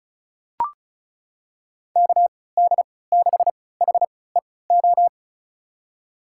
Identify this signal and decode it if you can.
Morse code, heard as a keyed tone, about 36 wpm, spaced as for 20 wpm, 700 Hz. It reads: KD6HEO
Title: USA Call Signs - 6 Characters - 3x Character Spacing - 35wpm